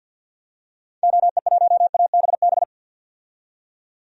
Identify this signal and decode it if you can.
Morse code, heard as a keyed tone, about 51 wpm, 700 Hz. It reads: OE1ABB
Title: DXCC-100 Call Signs - 6 Characters - 50wpm